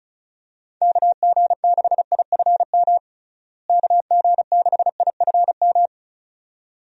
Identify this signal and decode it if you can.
Morse code, heard as a keyed tone, about 36 wpm, 700 Hz. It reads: KG6IFM KG6IFM
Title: USA Call Signs - 6 Characters - Repeated 2x - 35wpm